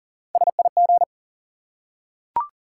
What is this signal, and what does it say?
Morse code, heard as a keyed tone, about 40 wpm, 700 Hz. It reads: SIG
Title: Top 180 QSO Elements 40wpm